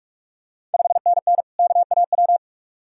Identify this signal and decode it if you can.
Morse code, heard as a keyed tone, about 46 wpm, 700 Hz. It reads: 5NN KAW
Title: Ontario QSO Party Exchange 45wpm